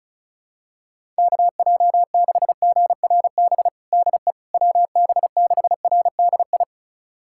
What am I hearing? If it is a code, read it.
Morse code, heard as a keyed tone, about 36 wpm, 700 Hz. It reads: KJ6GRB DE WB6RDI
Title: USA Callsign DE Callsign - 6 Characters - 35wpm